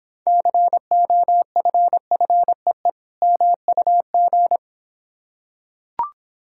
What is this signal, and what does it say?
Morse code, heard as a keyed tone, about 25 wpm, 700 Hz. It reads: COFFEE MUG